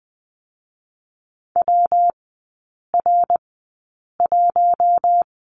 Morse code, heard as a keyed tone, about 20 wpm, 700 Hz. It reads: W R 1
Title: 114 Lesson 19e - Letter w - Three Characters 20wpm